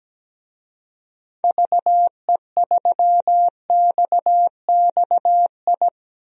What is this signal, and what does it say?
Morse code, heard as a keyed tone, about 17 wpm, 700 Hz. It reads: VE3XXI
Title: DXCC-100 Call Signs - 6 Characters - 17wpm